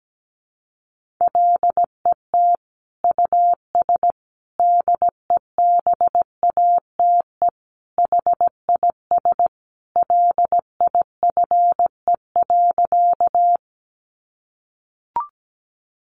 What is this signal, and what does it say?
Morse code, heard as a keyed tone, about 17 wpm, 700 Hz. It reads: LET US DEBATE HIS LIFE.
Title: Sentences from Top 1100 Words 17wpm